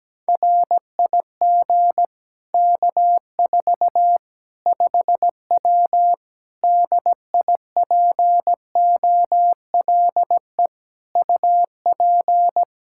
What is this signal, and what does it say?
Morse code, heard as a keyed tone, about 17 wpm, 700 Hz. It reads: RIG K4 5W DIPOLE UP
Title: Practice QSO - Exchange 2 - Info - 17wpm